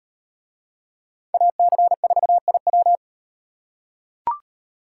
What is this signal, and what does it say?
Morse code, heard as a keyed tone, about 38 wpm, 700 Hz. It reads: AC4IW